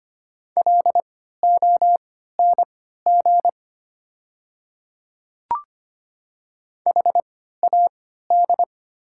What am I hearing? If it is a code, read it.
Morse code, heard as a keyed tone, about 25 wpm, spaced as for 14 wpm, 690 Hz. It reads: LONG HAD
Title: 3x Character Spacing - Top 100 Words - 25wpm